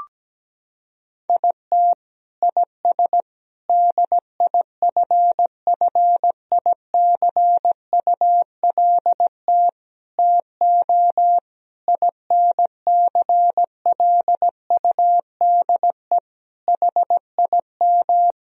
Morse code, heard as a keyed tone, about 17 wpm, 700 Hz. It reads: IT IS DIFFICULT TO INCLUDE HIM
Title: Sentences from Top 1000 Words - Review Part 3 - 17wpm